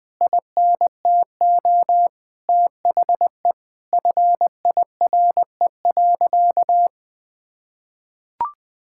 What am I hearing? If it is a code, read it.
Morse code, heard as a keyed tone, about 20 wpm, 700 Hz. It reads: INTO THE FIRE.
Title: Sentences with 9-Word Phrases - 20wpm